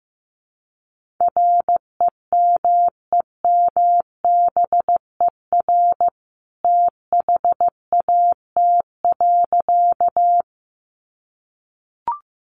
Morse code, heard as a keyed tone, about 15 wpm, 700 Hz. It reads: REMEMBER THAT.
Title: A Sound Of Thunder 15wpm